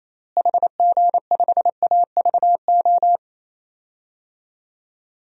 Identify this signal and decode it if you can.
Morse code, heard as a keyed tone, about 28 wpm, 700 Hz. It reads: HG5AVO
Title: DXCC-100 Call Signs - 6 Characters - Repeated 2x - 28wpm